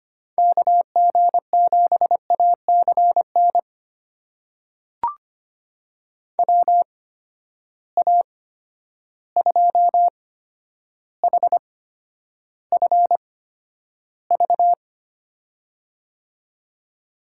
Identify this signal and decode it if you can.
Morse code, heard as a keyed tone, about 25 wpm, spaced as for 11 wpm, 700 Hz. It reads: KG7ACN WA2HFV
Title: USA Call Signs - 6 Characters - 8x Character Spacing - 25wpm